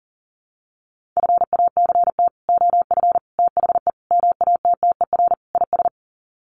Morse code, heard as a keyed tone, about 40 wpm, 700 Hz. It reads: FACT OF THE MATTER IS
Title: Sets of 5 Words 40wpm